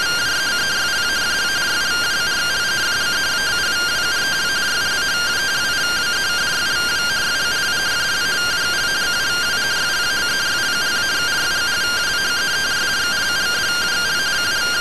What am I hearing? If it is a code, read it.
Two-tone teleprinter signal, about 170 Hz shift.